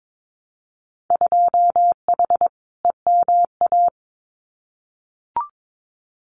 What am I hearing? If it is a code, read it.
Morse code, heard as a keyed tone, about 22 wpm, 700 Hz. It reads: 2H EMA
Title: Winter Field Day Exchanges 22wpm